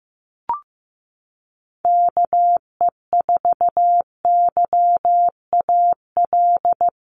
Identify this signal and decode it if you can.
Morse code, heard as a keyed tone, about 15 wpm, 700 Hz. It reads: KE4YAL
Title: USA Call Signs Repeated 3x 15wpm